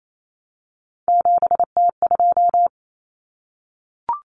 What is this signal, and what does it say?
Morse code, heard as a keyed tone, about 28 wpm, 700 Hz. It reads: 7T2
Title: Common Cut Numbers - 3-Digit 28wpm